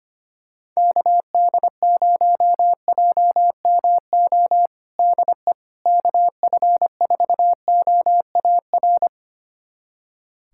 Morse code, heard as a keyed tone, about 25 wpm, 700 Hz. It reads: KD0JMO DE KF4OAR
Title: USA Callsign DE Callsign - 6 Characters - 25wpm